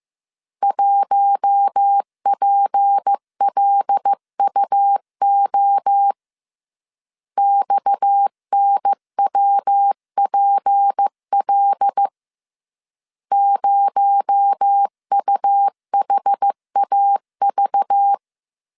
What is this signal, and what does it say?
Morse code, heard as a keyed tone, about 15 wpm, 800 Hz. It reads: 1PLUO XNWPL 0UHAV
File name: CW / DET / 1 6 1500C S C 14wpm 2951